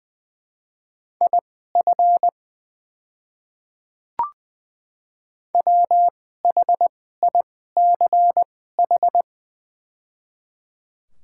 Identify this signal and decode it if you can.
Morse code, heard as a keyed tone, about 20 wpm, spaced as for 14 wpm, 700 Hz. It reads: IF WHICH
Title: 2x Character Spacing - Top 100 Words - 20wpm